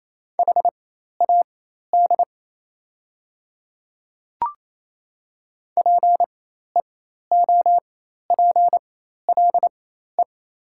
Morse code, heard as a keyed tone, about 28 wpm, spaced as for 13 wpm, 700 Hz. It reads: HAD PEOPLE
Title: 4x Character Spacing - Top 100 Words - 28wpm